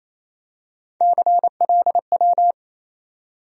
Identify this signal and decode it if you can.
Morse code, heard as a keyed tone, about 27 wpm, 700 Hz. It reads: CLW
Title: USA Call Sign Postfixes - 3 Letters 28wpm